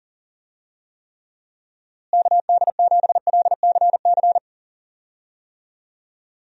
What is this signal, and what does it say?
Morse code, heard as a keyed tone, about 40 wpm, 700 Hz. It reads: KD7LCC
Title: USA Call Signs - 6 Characters - Repeated 2x - 40wpm